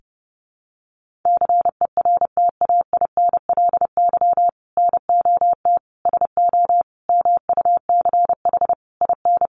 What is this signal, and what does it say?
Morse code, heard as a keyed tone, about 30 wpm, 700 Hz. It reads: CERTAINLY NOT SO MUCH IN
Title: Sentences from Top 1500 Words 30wpm